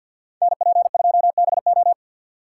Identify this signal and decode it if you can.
Morse code, heard as a keyed tone, about 65 wpm, 700 Hz. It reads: NP2BO